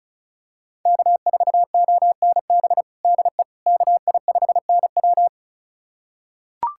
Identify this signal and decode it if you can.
Morse code, heard as a keyed tone, about 36 wpm, 700 Hz. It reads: K4ONB DE KI5NW
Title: USA Callsign DE Callsign 35wpm